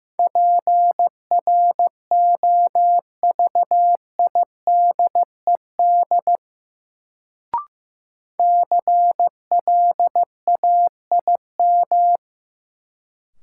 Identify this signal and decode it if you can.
Morse code, heard as a keyed tone, about 15 wpm, 700 Hz. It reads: PROVIDED CLAIM
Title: Sentences from Top 1500 Words - New Word Drill - 15wpm